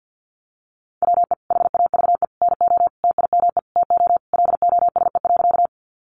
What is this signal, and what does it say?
Morse code, heard as a keyed tone, about 50 wpm, 700 Hz. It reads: WE HAVE NO TIME TO LOSE.
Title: Sentences from Top 1100 Words 50wpm